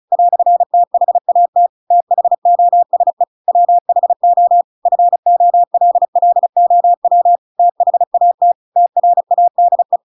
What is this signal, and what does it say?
Morse code, heard as a keyed tone, about 34 wpm, 700 Hz. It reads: "THAT THOSE WHO FOLLOW THAT TRADE